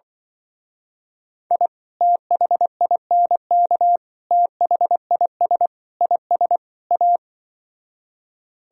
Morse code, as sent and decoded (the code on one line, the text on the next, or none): .. / - .... .. -. -.- / - .... .. ... / .. ... / .-
I THINK THIS IS A